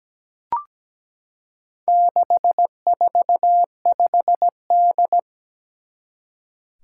Morse code, heard as a keyed tone, about 17 wpm, 700 Hz. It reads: 645D
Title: Four Challenging Letter-Number Combinations 17wpm